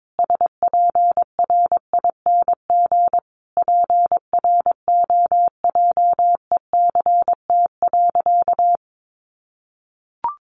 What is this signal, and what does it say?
Morse code, heard as a keyed tone, about 22 wpm, 700 Hz. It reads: SPRING PROJECT.
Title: Sentences from Top 1400 Words 22wpm